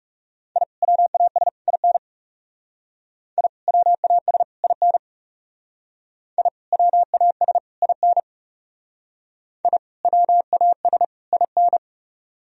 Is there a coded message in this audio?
Yes: Morse code from 0.6 to 11.8 s, heard as a keyed tone, about 40 wpm, 700 Hz. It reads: I WAS IN I WAS IN I WAS IN I WAS IN